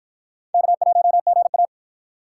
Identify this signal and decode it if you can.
Morse code, heard as a keyed tone, about 52 wpm, 700 Hz. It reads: K1GA